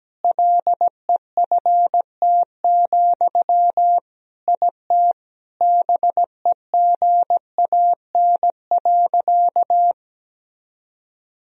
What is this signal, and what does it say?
Morse code, heard as a keyed tone, about 17 wpm, 700 Hz. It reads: LEFT, IT BEGAN.
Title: Sentences from Top 400 Words and Abbreviations 17wpm